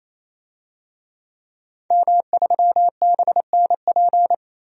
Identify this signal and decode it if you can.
Morse code, heard as a keyed tone, about 28 wpm, 700 Hz. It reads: M3BNP
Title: DXCC-100 Call Signs - 5 Characters - 28wpm